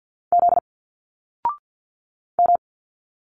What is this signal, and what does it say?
Morse code, heard as a keyed tone, about 50 wpm, 700 Hz. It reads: Z M
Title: Single Letters 50wpm